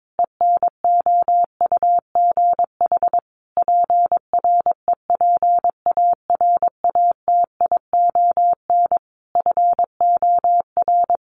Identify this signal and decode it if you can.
Morse code, heard as a keyed tone, about 22 wpm, 700 Hz. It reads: ENOUGH PREPARATION FOR
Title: Sentences from Top 1400 Words 22wpm